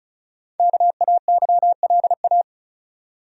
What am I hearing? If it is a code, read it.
Morse code, heard as a keyed tone, about 36 wpm, 700 Hz. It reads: KAYLA
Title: USA Common Names - 35wpm